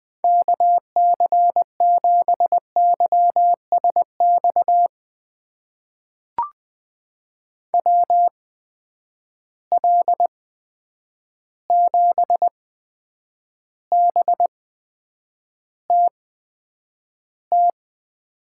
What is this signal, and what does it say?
Morse code, heard as a keyed tone, about 20 wpm, spaced as for 9 wpm, 700 Hz. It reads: KC7YSX WL7BTT
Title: USA Call Signs - 6 Characters - 8x Character Spacing - 20wpm